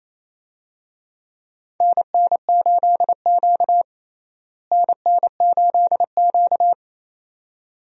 Morse code, heard as a keyed tone, about 28 wpm, 700 Hz. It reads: NN8Q NN8Q
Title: USA Call Signs - 4 Characters - Repeated 2x - 28wpm